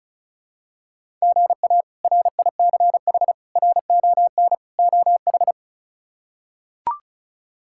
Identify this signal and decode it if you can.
Morse code, heard as a keyed tone, about 36 wpm, 700 Hz. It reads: GA RICH RON OH
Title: K1USN SST QSO Practice - 35wpm